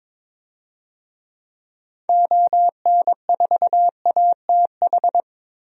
Morse code, heard as a keyed tone, about 22 wpm, 700 Hz. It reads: ON4ATH
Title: DXCC-100 Call Signs - 6 Characters - Repeated 2x - 22wpm